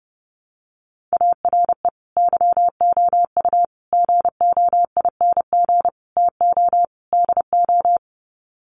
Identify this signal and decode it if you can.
Morse code, heard as a keyed tone, about 30 wpm, 700 Hz. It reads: ARE YOU GOING TO DO
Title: Sets of 5 Words 30wpm